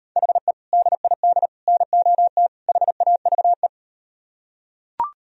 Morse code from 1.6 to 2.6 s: NOT

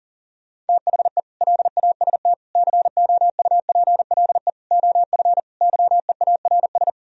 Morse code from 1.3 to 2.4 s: LAST

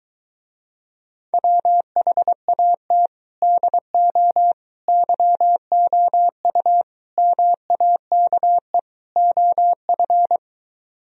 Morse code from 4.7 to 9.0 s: YOU MAKE

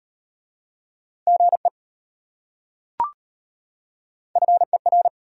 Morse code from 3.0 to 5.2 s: FER

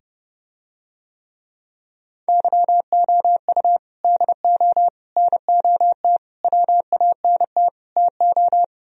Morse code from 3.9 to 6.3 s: DO NOT

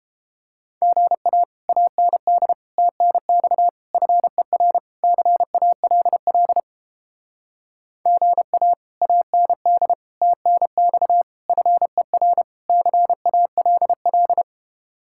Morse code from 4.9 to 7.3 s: CALL

